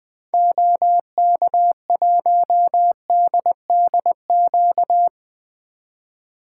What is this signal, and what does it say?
Morse code, heard as a keyed tone, about 21 wpm, 700 Hz. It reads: OK1DDQ
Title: 2023 SOTA Activator Call Signs - 20wpm